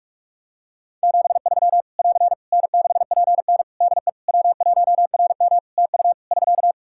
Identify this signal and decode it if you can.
Morse code, heard as a keyed tone, about 46 wpm, 700 Hz. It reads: 73 <AR> N6PN DE W1RM TU <SK>
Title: Practice QSO - Exchange 4 - Ending - 45wpm